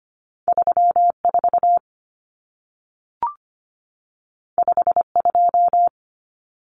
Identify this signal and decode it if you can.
Morse code, heard as a keyed tone, about 25 wpm, 700 Hz. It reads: 34 52
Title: Numbers - 2-Digit 25wpm